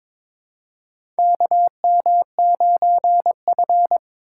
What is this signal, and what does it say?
Morse code, heard as a keyed tone, about 23 wpm, 700 Hz. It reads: KM9F
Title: USA Call Signs - 4 Characters - 3x Character Spacing - 22wpm